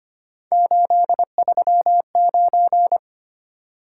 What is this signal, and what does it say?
Morse code, heard as a keyed tone, about 25 wpm, 700 Hz. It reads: 839
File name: Three Letter-Number Combinations - 25wpm